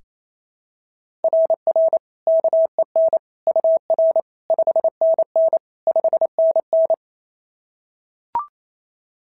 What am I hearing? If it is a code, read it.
Morse code, heard as a keyed tone, about 28 wpm, 690 Hz. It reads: RR KEN UR 5NN 5NN
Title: SOTA QSO Practice 28wpm